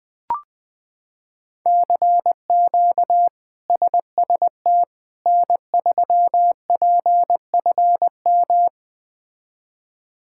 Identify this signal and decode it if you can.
Morse code, heard as a keyed tone, about 20 wpm, 700 Hz. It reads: CQ SST N3PFM
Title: K1USN SST QSO Practice - 20wpm